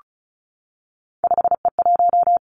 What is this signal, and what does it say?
Morse code, heard as a keyed tone, about 36 wpm, 700 Hz. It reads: 5E1
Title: Three Challenging Letter-Number Combinations 35wpm